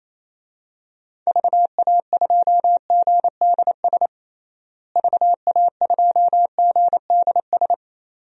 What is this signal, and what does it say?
Morse code, heard as a keyed tone, about 28 wpm, 700 Hz. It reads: VA2GDS VA2GDS
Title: DXCC-100 Call Signs - 6 Characters - Repeated 2x - 28wpm